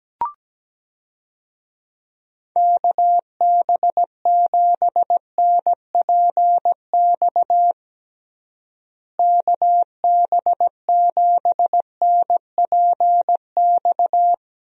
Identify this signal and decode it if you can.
Morse code, heard as a keyed tone, about 17 wpm, 700 Hz. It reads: KB7NPX KB7NPX
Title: USA Call Signs - 6 Characters - Repeated 3x - 17wpm